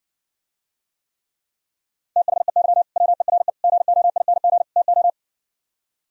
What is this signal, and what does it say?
Morse code, heard as a keyed tone, about 60 wpm, 700 Hz. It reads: THEY WERE GOING TO